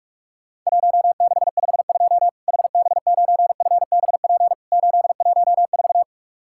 Morse code, heard as a keyed tone, about 41 wpm, 700 Hz. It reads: 1652 HB9FBP 814
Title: Worked All Europe DX Contest - QTC Exchange - 45wpm